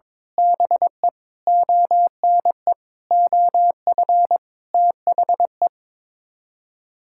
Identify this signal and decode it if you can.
Morse code, heard as a keyed tone, about 21 wpm, 700 Hz. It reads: BE ONE OF THE
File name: practice / 22wpm / Sets of 7 Words - 22wpm